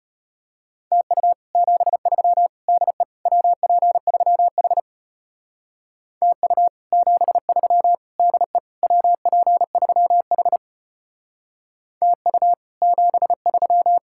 Morse code, heard as a keyed tone, about 31 wpm, 700 Hz. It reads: TU 73 DE WP3H TU 73 DE WP3H TU 73
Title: Speed-Racing Common QSO Phrases - 25wpm